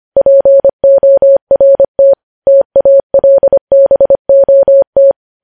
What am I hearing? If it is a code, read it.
Morse code, heard as a keyed tone, about 25 wpm, 550 Hz. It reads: PORT TALBOT
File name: podcast / News headlines-1x-25x25